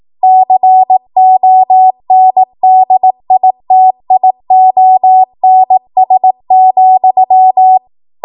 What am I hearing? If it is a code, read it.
Morse code, heard as a keyed tone, about 18 wpm, 750 Hz. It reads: CONDITIONS,
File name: Morse / 18WPM